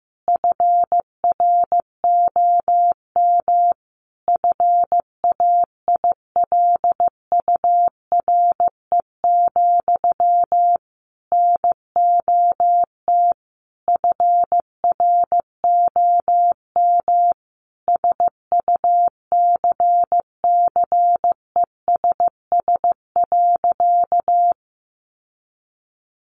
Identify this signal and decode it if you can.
Morse code, heard as a keyed tone, about 15 wpm, 700 Hz. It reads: FROM FAILURE, NOT FROM SUCCESS.
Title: Sentences from Top 900 Words 15wpm